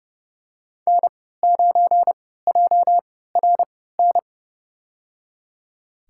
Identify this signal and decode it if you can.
Morse code, heard as a keyed tone, about 30 wpm, spaced as for 17 wpm, 700 Hz. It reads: N9JRN
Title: USA Call Signs - 5 Characters - 3x Character Spacing - 30wpm